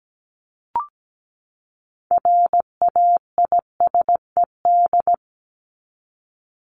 Morse code, heard as a keyed tone, about 17 wpm, 700 Hz. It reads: RAISED
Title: ED Suffix - Top 100 Words with Base Words 17wpm